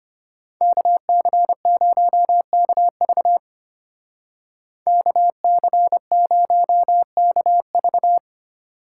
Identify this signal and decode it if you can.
Morse code, heard as a keyed tone, about 32 wpm, 700 Hz. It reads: KC0KV KC0KV